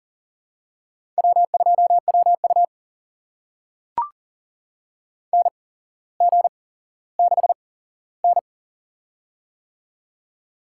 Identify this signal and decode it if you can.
Morse code, heard as a keyed tone, about 43 wpm, spaced as for 17 wpm, 700 Hz. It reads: W2WU NG6N
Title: USA Call Signs - 8x Character Spacing - 40wpm